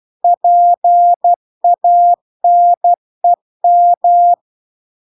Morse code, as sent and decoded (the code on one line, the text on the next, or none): .--. .- -. --
PANEM